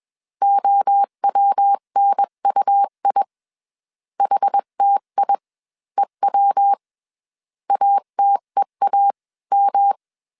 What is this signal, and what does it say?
Morse code, heard as a keyed tone, about 21 wpm, spaced as for 18 wpm, 800 Hz. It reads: OWNUI HTIEW ATEAM
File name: CW / DET / 1 3 1500C S B 20wpm 1754